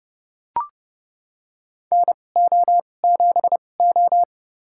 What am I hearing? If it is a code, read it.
Morse code, heard as a keyed tone, about 30 wpm, spaced as for 22 wpm, 700 Hz. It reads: NO7O
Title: USA Call Signs - 4 Characters - 2x Character Spacing - 30wpm